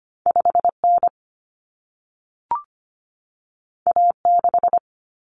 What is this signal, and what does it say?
Morse code, heard as a keyed tone, about 25 wpm, 700 Hz. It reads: 5N A6
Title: Common Cut Numbers - 2-Digit 25wpm